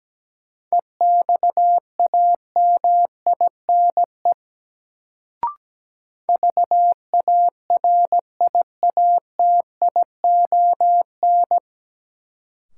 Morse code, heard as a keyed tone, about 17 wpm, 700 Hz. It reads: EXAMINE VARIATION
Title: Sentences from Top 1600 Words - New Word Drill - 17wpm